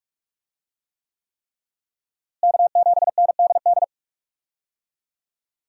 Morse code, heard as a keyed tone, about 46 wpm, 700 Hz. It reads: K7NDD
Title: USA Call Signs - 5 Characters - Repeated 3x - 45wpm